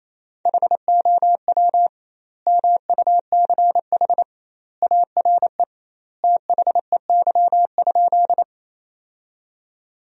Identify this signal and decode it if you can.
Morse code, heard as a keyed tone, about 28 wpm, 700 Hz. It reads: HOW MUCH ARE THEY?
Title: 2x Word-Spacing - Sentences from Top 200 Words - 28wpm